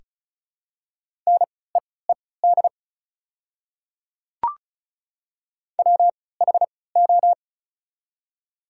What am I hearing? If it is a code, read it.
Morse code, heard as a keyed tone, about 36 wpm, spaced as for 20 wpm, 700 Hz. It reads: NEED WHO
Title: 3x Character Spacing - Top 300 Words - 35wpm